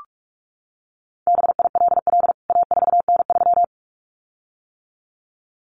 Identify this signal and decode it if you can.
Morse code, heard as a keyed tone, about 44 wpm, 700 Hz. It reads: BILL A4N3
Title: CWT Exchange 45wpm